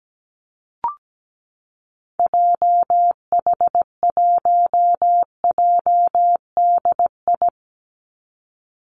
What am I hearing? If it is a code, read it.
Morse code, heard as a keyed tone, about 17 wpm, 700 Hz. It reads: JH1JDI
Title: Call Signs - 17wpm